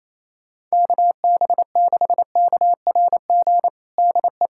Morse code, heard as a keyed tone, about 28 wpm, 700 Hz. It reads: KB6KRG DE
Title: USA Callsign DE Callsign - 6 Characters - 28wpm